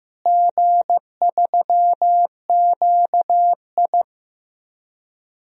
Morse code, heard as a keyed tone, about 15 wpm, 700 Hz. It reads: G3QI
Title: DXCC-100 Call Signs - 4 Characters - Repeated 3x - 15wpm